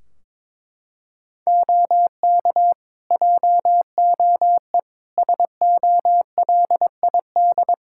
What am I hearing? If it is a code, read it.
Morse code, heard as a keyed tone, about 22 wpm, 700 Hz. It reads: OK JOE SOLID
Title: K1USN SST QSO Practice 22wpm